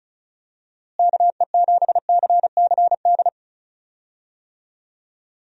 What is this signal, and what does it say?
Morse code, heard as a keyed tone, about 36 wpm, 700 Hz. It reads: KE7CCD
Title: USA Call Signs - 6 Characters - Repeated 2x - 35wpm